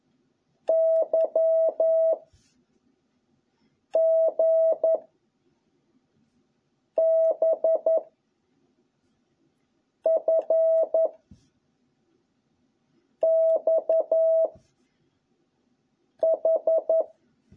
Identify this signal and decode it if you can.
Morse code, heard as a keyed tone, about 11 wpm, 650 Hz. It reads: Y G B F X H